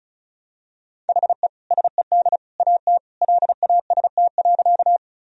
Morse code, heard as a keyed tone, about 36 wpm, 700 Hz. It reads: HE SED AT LAST.
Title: Sentences from Top 300 Words and Abbreviations 35wpm